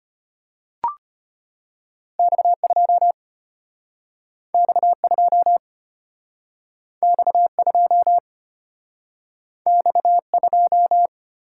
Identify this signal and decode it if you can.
Morse code, heard as a keyed tone, about 34 wpm, 700 Hz. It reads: X2 X2 X2 X2